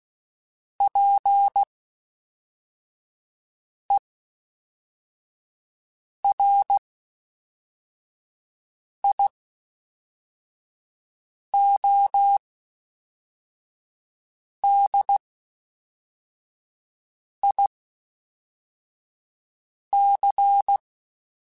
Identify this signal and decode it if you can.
Morse code, heard as a keyed tone, about 16 wpm, spaced as for 4 wpm, 800 Hz. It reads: PERIODIC
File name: (spaced) / FromSMs spaced mp3s / Antennas sp@5